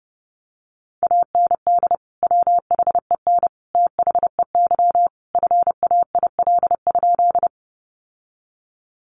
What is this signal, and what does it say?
Morse code, heard as a keyed tone, about 30 wpm, 700 Hz. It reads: AND WHEN THEY FAIL?
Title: Sentences from Top 1100 Words 30wpm